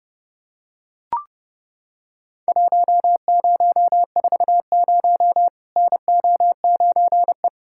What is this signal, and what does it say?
Morse code, heard as a keyed tone, about 30 wpm, 700 Hz. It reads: 1040 NO9E
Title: Worked All Europe DX Contest - QTC Exchange - 30wpm